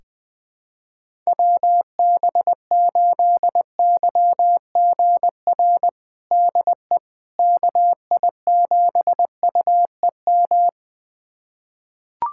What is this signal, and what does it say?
Morse code, heard as a keyed tone, about 20 wpm, 700 Hz. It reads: WB8YGR DE KI7UEM
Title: USA Callsign DE Callsign - 6 Characters - 20wpm